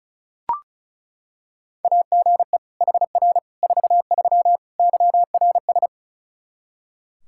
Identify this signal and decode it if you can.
Morse code, heard as a keyed tone, about 34 wpm, 700 Hz. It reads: AGE HR 43 YRS
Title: Practice QSO - Exchange 3 - More Info - 35wpm